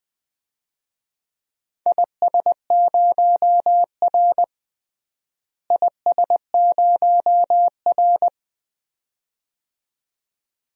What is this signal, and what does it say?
Morse code, heard as a keyed tone, about 20 wpm, 700 Hz. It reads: IS0R IS0R
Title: DXCC-100 Call Signs - 4 Characters - Repeated 2x - 20wpm